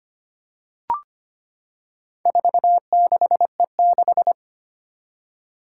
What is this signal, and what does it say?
Morse code, heard as a keyed tone, about 25 wpm, 700 Hz. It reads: 46E6
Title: Four Challenging Letter-Number Combinations 25wpm